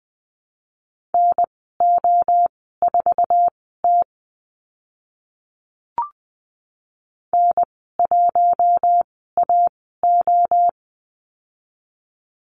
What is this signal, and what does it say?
Morse code, heard as a keyed tone, about 20 wpm, spaced as for 14 wpm, 700 Hz. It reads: NO4T N1AO
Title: 062 Lesson 11g - Letter h - Call Signs 20wpm